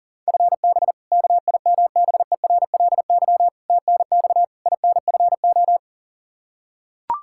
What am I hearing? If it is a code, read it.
Morse code, heard as a keyed tone, about 40 wpm, 700 Hz. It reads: FB KIMBERLY TNX INFO
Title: Practice QSO - Exchange 3 - More Info - 40wpm